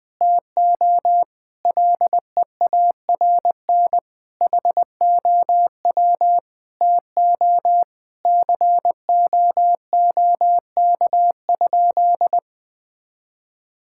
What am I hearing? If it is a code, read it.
Morse code, heard as a keyed tone, about 21 wpm, 700 Hz. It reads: TO LEARN HOW TO COOK?
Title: Sentences with 6-Word Phrases - Spoken Beginning - 20wpm